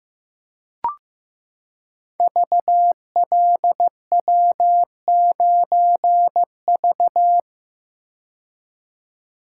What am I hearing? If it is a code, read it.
Morse code, heard as a keyed tone, about 15 wpm, 700 Hz. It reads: VLW9V
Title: Code Groups - 5 Letter:Numbers - 15wpm